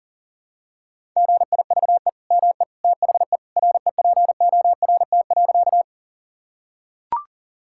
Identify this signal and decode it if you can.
Morse code, heard as a keyed tone, about 40 wpm, 700 Hz. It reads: GIVE ME THE REPORT.
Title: Sentences from Top 1000 Words - Review Part 6 - 40wpm